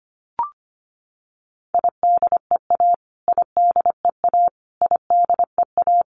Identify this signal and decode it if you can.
Morse code, heard as a keyed tone, about 25 wpm, 700 Hz. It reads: IDEA IDEA IDEA
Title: Top 500 Words Repeated 25wpm